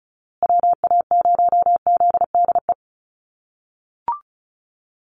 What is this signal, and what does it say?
Morse code, heard as a keyed tone, about 34 wpm, 700 Hz. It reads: WA0ZDE
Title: USA Call Signs Repeated 3x 35wpm